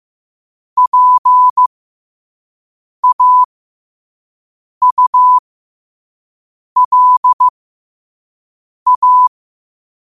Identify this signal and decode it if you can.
Morse code, heard as a keyed tone, about 15 wpm, spaced as for 5 wpm, 1000 Hz. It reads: PAULA
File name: qotdfiles / 5wpm